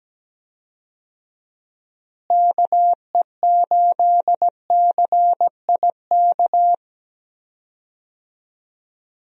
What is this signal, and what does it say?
Morse code, heard as a keyed tone, about 17 wpm, 700 Hz. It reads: KE8CIK
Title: USA Call Signs - 6 Characters - Repeated 3x - 17wpm